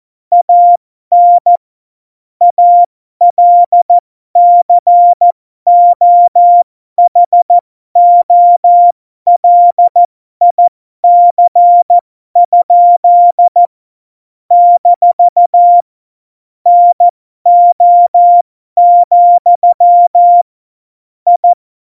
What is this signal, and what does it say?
Morse code, heard as a keyed tone, about 14 wpm, spaced as for 12 wpm, 700 Hz. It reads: AN ALCOHOLIC? - NO, I
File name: Morsekurs / Witze